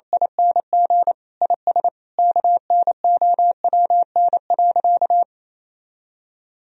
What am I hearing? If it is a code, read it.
Morse code, heard as a keyed tone, about 28 wpm, 700 Hz. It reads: ING IS KNOWN.
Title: Sentences from Top 1000 Words - Review Part 4 - 28wpm